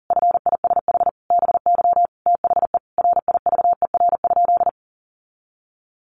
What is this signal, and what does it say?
Morse code, heard as a keyed tone, about 40 wpm, 700 Hz. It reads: FISH BY THE RIVER?
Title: Sentences from Top 1400 Words 40wpm